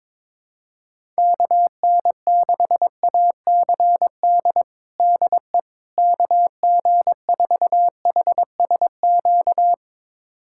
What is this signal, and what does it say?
Morse code, heard as a keyed tone, about 22 wpm, 700 Hz. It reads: KN6ACD DE KG4HSQ
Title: USA Callsign DE Callsign 22wpm